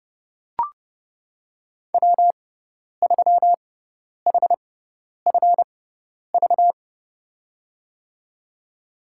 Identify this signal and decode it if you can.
Morse code, heard as a keyed tone, about 30 wpm, spaced as for 10 wpm, 700 Hz. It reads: W3HFV
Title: USA Call Signs - 6x Character Spacing - 30wpm